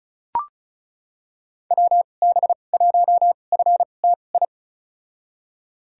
Morse code, heard as a keyed tone, about 36 wpm, spaced as for 25 wpm, 700 Hz. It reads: WB1FTI
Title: USA Call Signs - 6 Characters - 2x Character Spacing - 35wpm